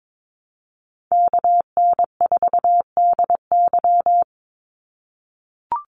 Morse code, heard as a keyed tone, about 22 wpm, 700 Hz. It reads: KN4DY
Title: USA Call Signs 22wpm